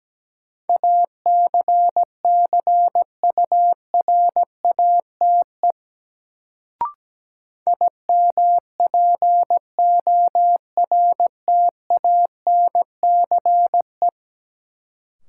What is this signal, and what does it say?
Morse code, heard as a keyed tone, about 17 wpm, 700 Hz. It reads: ACCURATE IMPORTANCE
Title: Sentences from Top 1200 Words - New Word Drill Alt - 17wpm